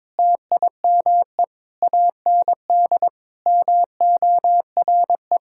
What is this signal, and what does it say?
Morse code, heard as a keyed tone, about 22 wpm, 700 Hz. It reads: TIME AND MORE
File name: practice / 22wpm / Sentences with 2-Word Phrases - 22wpm